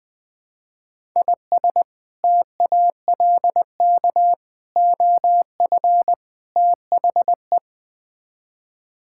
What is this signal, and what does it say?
Morse code, heard as a keyed tone, about 21 wpm, 700 Hz. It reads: IS TALK OF THE
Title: Build A Phrase - 5-Word Phrases - 20wpm